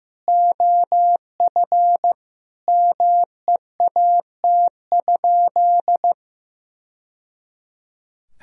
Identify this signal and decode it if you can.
Morse code, heard as a keyed tone, about 15 wpm, 700 Hz. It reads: OF MEAT?
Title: Sentences from Top 1000 Words - Review Part 3 - 15wpm